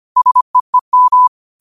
Morse code, heard as a keyed tone, about 25 wpm, 1000 Hz. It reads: SEEM
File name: qotdfiles / 25wpm